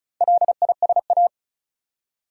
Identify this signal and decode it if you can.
Morse code, heard as a keyed tone, about 34 wpm, 700 Hz. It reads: LISA
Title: USA Common Names - Max 4 Letters - 35wpm